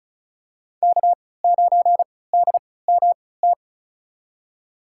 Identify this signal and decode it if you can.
Morse code, heard as a keyed tone, about 34 wpm, spaced as for 20 wpm, 700 Hz. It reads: K9DMT
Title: USA Call Signs - 5 Characters - 3x Character Spacing - 35wpm